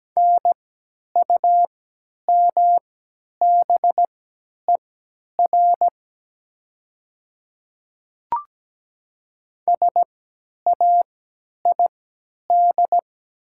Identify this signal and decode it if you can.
Morse code, heard as a keyed tone, about 17 wpm, spaced as for 10 wpm, 700 Hz. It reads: NUMBER SAID
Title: 3x Character Spacing - Top 100 Words - 17wpm